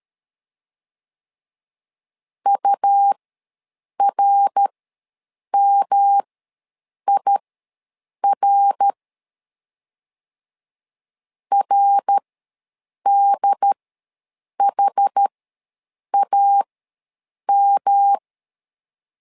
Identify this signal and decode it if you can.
Morse code, heard as a keyed tone, about 13 wpm, spaced as for 7 wpm, 800 Hz. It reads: URMIR RDHAM
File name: LEZIONI / 3 MWRO / 3 MWR ws 12wpm 1809